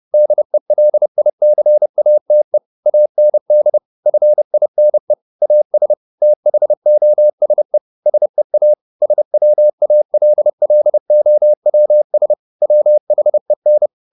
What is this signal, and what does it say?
Morse code, heard as a keyed tone, about 30 wpm, 600 Hz. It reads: DELICATE AND FINE AS THOSE SEA SWALLOWS WHEN
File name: the old man and the sea / the old man and the sea 1 30wpm